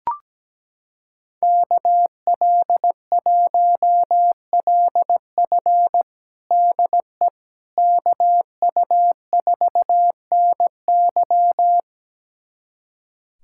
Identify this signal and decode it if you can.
Morse code, heard as a keyed tone, about 17 wpm, 700 Hz. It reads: KL1LF DE KU4NY
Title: USA Callsign DE Callsign - 5 Characters - 17wpm